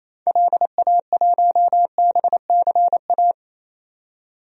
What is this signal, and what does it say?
Morse code, heard as a keyed tone, about 28 wpm, 700 Hz. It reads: LA1BCA
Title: DXCC-100 Call Signs - 6 Characters - 28wpm